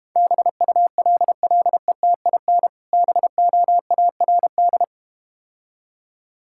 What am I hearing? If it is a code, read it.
Morse code, heard as a keyed tone, about 31 wpm, 700 Hz. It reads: BULLETIN BOARD
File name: Easy Words - Speed-Racing with Hints - 25wpm